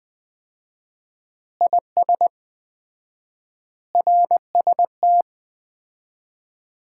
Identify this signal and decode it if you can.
Morse code, heard as a keyed tone, about 20 wpm, 700 Hz. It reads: IS RST
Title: Common QSO Phrases - 4x Word-Spacing - 20wpm